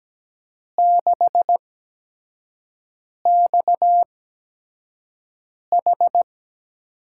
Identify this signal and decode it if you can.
Morse code, heard as a keyed tone, about 17 wpm, spaced as for 5 wpm, 700 Hz. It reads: 6XH
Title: USA Call Signs - 5 Characters - 8x Character Spacing - 17wpm